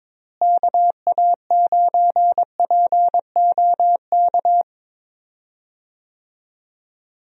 Morse code, heard as a keyed tone, about 22 wpm, 700 Hz. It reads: KA9POK
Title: USA Call Signs - 6 Characters - Repeated 3x - 22wpm